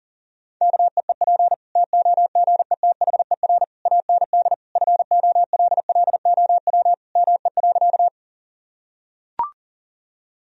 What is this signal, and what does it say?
Morse code, heard as a keyed tone, about 37 wpm, 700 Hz. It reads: KEEP TOGETHER AND FOLLOW ME.
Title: Sentences from Top 1000 Words - Review Part 1 - 40wpm